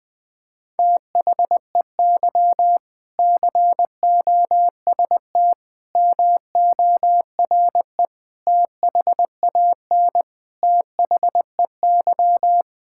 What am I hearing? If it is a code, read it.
Morse code, heard as a keyed tone, about 21 wpm, 700 Hz. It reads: THEY COST MORE THAN THEY
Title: Sentences from Top 1000 Words - Review Part 5 - 20wpm